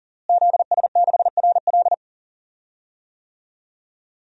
Morse code, heard as a keyed tone, about 40 wpm, 700 Hz. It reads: ZS6RL
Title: DXCC-100 Call Signs - 5 Characters - Repeated 3x - 40wpm